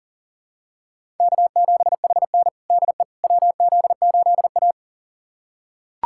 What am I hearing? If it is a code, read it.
Morse code, heard as a keyed tone, about 40 wpm, 700 Hz. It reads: K7HN DE WZ8A
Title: USA Callsign DE Callsign - 4 Characters - 40wpm